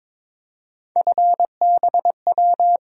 Morse code, heard as a keyed tone, about 22 wpm, 700 Hz. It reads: FBW
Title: USA Call Sign Postfixes - 3 Letters 22wpm